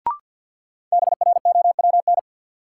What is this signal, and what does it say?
Morse code, heard as a keyed tone, about 53 wpm, 700 Hz. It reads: BROWN
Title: Sentences from Top 1200 Words - New Word Drill - 50wpm